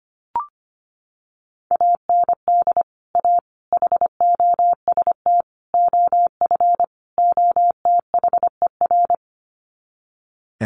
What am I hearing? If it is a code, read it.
Morse code, heard as a keyed tone, about 25 wpm, 700 Hz. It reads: AND A HOST OF OTHER
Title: Sets of 5 Words Encore2 25wpm